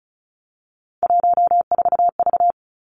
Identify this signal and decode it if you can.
Morse code, heard as a keyed tone, about 36 wpm, 700 Hz. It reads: 14V
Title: Three Challenging Letter-Number Combinations 35wpm